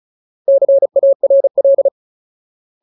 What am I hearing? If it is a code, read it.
Morse code, heard as a keyed tone, about 34 wpm, 550 Hz. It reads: CARL
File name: Ultimate Word List / 35WPM Ultimate